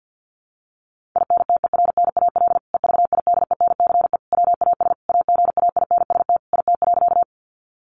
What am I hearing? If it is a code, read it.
Morse code, heard as a keyed tone, about 50 wpm, 700 Hz. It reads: INTERNAL EVIDENCE WAS AGAINST IT.
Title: Sentences from Top 1400 Words 50wpm